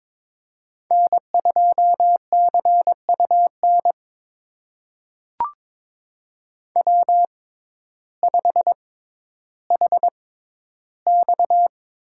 Morse code, heard as a keyed tone, about 22 wpm, spaced as for 12 wpm, 700 Hz. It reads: N2CUN W5HX
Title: USA Call Signs - 6x Character Spacing - 22wpm